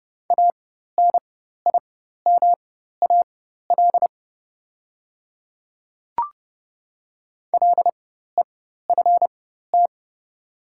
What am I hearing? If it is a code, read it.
Morse code, heard as a keyed tone, about 30 wpm, spaced as for 14 wpm, 690 Hz. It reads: ANIMAL LEFT
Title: 4x Character Spacing - Top 300 Words - 30wpm